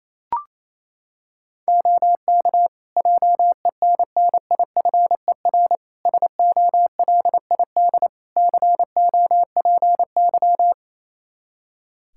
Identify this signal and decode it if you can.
Morse code, heard as a keyed tone, about 27 wpm, 700 Hz. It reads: OK JENNIFER SOLID COPY
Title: Practice QSO - Exchange 3 - More Info - 28wpm